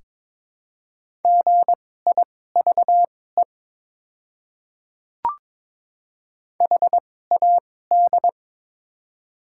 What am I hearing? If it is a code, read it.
Morse code, heard as a keyed tone, about 22 wpm, spaced as for 16 wpm, 700 Hz. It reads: GIVE HAD
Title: 2x Character Spacing - Top 200 Words - 22wpm